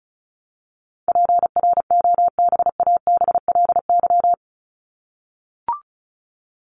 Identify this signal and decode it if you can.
Morse code, heard as a keyed tone, about 34 wpm, 700 Hz. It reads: PROBABLY
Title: LY Suffix - Top 100 Words with Base Words 35wpm